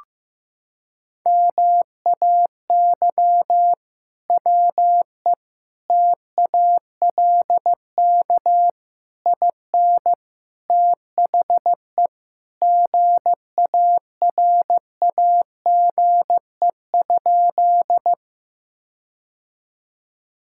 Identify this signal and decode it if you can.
Morse code, heard as a keyed tone, about 15 wpm, 700 Hz. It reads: MAY WE TALK IN THE GARAGE?
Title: Sentences from Top 1600 Words 15wpm